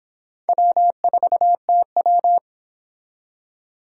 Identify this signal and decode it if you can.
Morse code, heard as a keyed tone, about 25 wpm, 700 Hz. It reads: W4TW